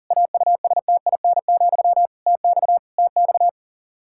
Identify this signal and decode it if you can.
Morse code, heard as a keyed tone, about 40 wpm, 700 Hz. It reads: AUSTIN, TX TX
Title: Practice QSO - Exchange 1 - Beginning - 40wpm